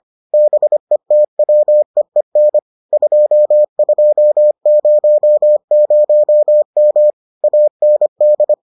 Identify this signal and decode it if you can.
Morse code, heard as a keyed tone, about 25 wpm, 600 Hz. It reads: BETWEEN 2200M AND